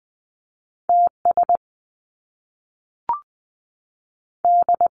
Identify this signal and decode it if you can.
Morse code, heard as a keyed tone, about 20 wpm, 700 Hz. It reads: T S D
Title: Top Two Letter Combinations in Words 20wpm